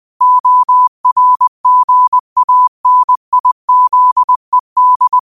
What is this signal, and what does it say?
Morse code, heard as a keyed tone, about 20 wpm, 1000 Hz. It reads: ORGANIZED
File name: qotdfiles / 20wpm